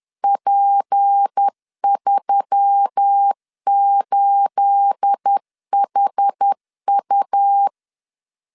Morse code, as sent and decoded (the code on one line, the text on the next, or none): .--. ...-- ---.. .... ..-
P38HU